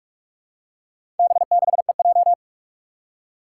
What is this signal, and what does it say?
Morse code, heard as a keyed tone, about 46 wpm, 700 Hz. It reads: B6EJ
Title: Four Challenging Letter-Number Combinations 45wpm